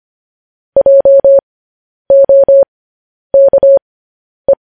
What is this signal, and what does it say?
Morse code, heard as a keyed tone, about 25 wpm, spaced as for 10 wpm, 550 Hz. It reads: JOKE